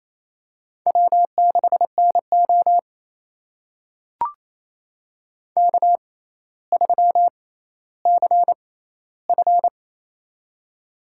Morse code, heard as a keyed tone, about 28 wpm, spaced as for 14 wpm, 700 Hz. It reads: W6NO K3CF
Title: USA Call Signs - 6x Character Spacing - 28wpm